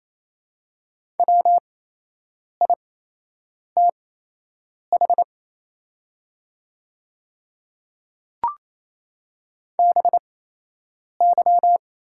Morse code, heard as a keyed tone, about 27 wpm, spaced as for 8 wpm, 700 Hz. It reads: WITH BY
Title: 8x Character Spacing - Top 100 Words - 28wpm